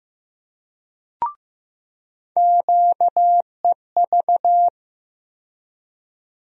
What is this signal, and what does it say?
Morse code, heard as a keyed tone, about 15 wpm, 700 Hz. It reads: QEV
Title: USA Call Sign Postfixes - 3 Letters 15wpm